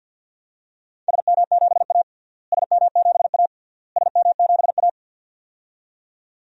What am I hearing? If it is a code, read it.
Morse code, heard as a keyed tone, about 52 wpm, 700 Hz. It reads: SM7A SM7A SM7A
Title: DXCC-100 Call Signs - 4 Characters - Repeated 3x - 50wpm